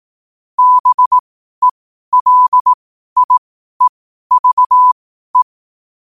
Morse code, heard as a keyed tone, about 18 wpm, spaced as for 12 wpm, 1000 Hz. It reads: BELIEVE
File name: qotdfiles / 12wpm